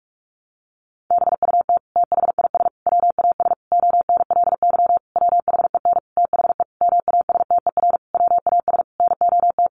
Morse code, heard as a keyed tone, about 44 wpm, 700 Hz. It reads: BUT THIS WAS ONLY WHEN THE MASTER WAS NOT
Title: Sentences from Top 1300 Words 45wpm